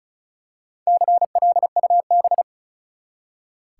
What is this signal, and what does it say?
Morse code, heard as a keyed tone, about 36 wpm, 700 Hz. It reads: CLUB
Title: Four Letter Words - 35wpm